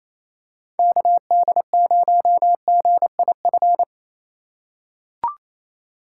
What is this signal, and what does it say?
Morse code, heard as a keyed tone, about 28 wpm, 700 Hz. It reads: KD0GIF
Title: USA Call Signs - 6 Characters - 2x Character Spacing - 28wpm